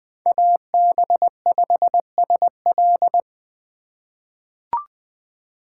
Morse code, heard as a keyed tone, about 20 wpm, 700 Hz. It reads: AB5SL
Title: USA Call Signs - 5 Characters - 2x Character Spacing - 20wpm